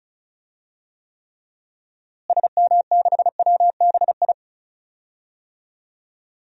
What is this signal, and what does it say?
Morse code, heard as a keyed tone, about 34 wpm, 700 Hz. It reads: SM6WBI
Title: DXCC-100 Call Signs - 6 Characters - Repeated 2x - 35wpm